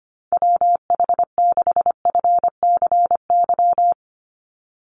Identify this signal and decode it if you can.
Morse code, heard as a keyed tone, about 25 wpm, 700 Hz. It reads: WH6FCY
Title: USA Call Signs 25wpm